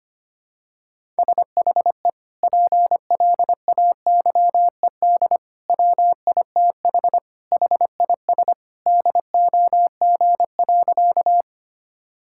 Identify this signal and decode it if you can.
Morse code, heard as a keyed tone, about 25 wpm, 700 Hz. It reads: SHE PLAYED WITH HIS DOG.
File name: practice / 25wpm / Sentences with 2-Word Phrases - 25wpm